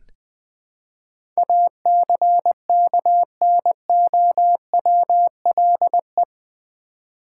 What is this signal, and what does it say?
Morse code, heard as a keyed tone, about 20 wpm, 700 Hz. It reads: ACKNOWLE
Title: Build A Word - 4 Letters At A Time - 11-Letter Words - 20wpm